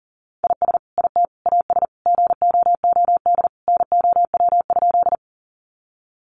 Morse code, heard as a keyed tone, about 40 wpm, 700 Hz. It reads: IS IT AS GOOD NOW?
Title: Sentences from Top 200 Words 40wpm